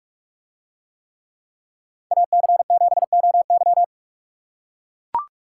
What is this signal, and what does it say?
Morse code, heard as a keyed tone, about 44 wpm, 700 Hz. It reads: AC7OY